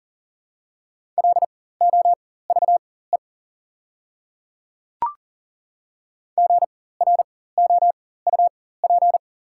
Morse code, heard as a keyed tone, about 40 wpm, spaced as for 19 wpm, 700 Hz. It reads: LOVE GROUP
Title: 4x Character Spacing - Top 500 Words - 40wpm